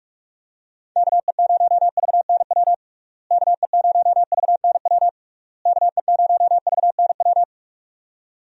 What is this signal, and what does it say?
Morse code, heard as a keyed tone, about 46 wpm, 700 Hz. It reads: KE0VNW KE0VNW KE0VNW
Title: USA Call Signs - 6 Characters - Repeated 3x - 45wpm